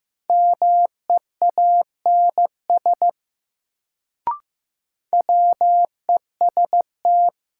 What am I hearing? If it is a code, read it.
Morse code, heard as a keyed tone, about 15 wpm, 700 Hz. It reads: MEANS WEST
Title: Sets of 5 Words - New Word Drill - 15wpm